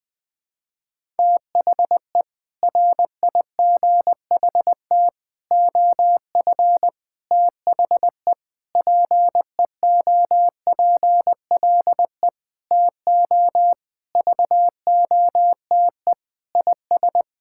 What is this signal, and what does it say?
Morse code, heard as a keyed tone, about 20 wpm, 700 Hz. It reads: THE RIGHT OF THE PEOPLE TO VOTE IS
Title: Sentences with 6-Word Phrases - Spoken Beginning - 20wpm